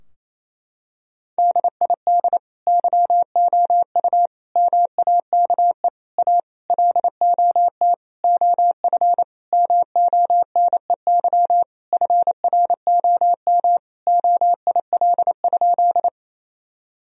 Morse code, heard as a keyed tone, about 28 wpm, 700 Hz. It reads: DID YOU MAKE A LOT OF MONEY FROM OIL?
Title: Sentences from Top 1000 Words - Review Part 4 - 28wpm